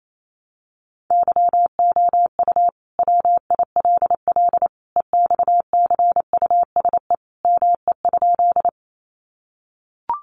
Morse code, heard as a keyed tone, about 28 wpm, 700 Hz. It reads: YOU WILL EXCUSE ME?
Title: Sentences from Top 700 Words 28wpm